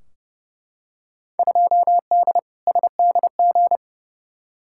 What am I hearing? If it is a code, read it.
Morse code, heard as a keyed tone, about 31 wpm, 700 Hz. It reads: 2D SDG
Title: Field Day Exchanges Encore - 30wpm